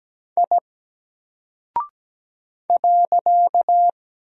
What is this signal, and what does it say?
Morse code, heard as a keyed tone, about 17 wpm, 700 Hz. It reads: I .
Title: Single Character (Letter:Number:Punctuation:Prosign) Alt 17wpm